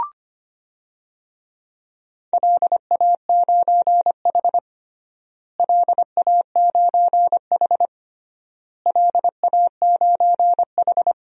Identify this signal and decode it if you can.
Morse code, heard as a keyed tone, about 25 wpm, 700 Hz. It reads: LA9H LA9H LA9H
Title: DXCC-100 Call Signs - 4 Characters - Repeated 3x - 25wpm